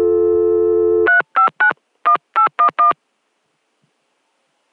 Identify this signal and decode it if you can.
DTMF, heard as telephone touch tones, tones of about 100 ms. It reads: 3591811